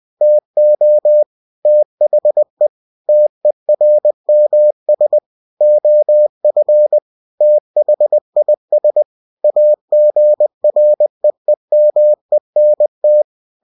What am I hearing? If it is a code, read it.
Morse code, heard as a keyed tone, about 20 wpm, 600 Hz. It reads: TO THE TERMS OF THIS AGREEMENT
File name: cw-books / winnie-the-pooh / winnie-20wpm-page-0114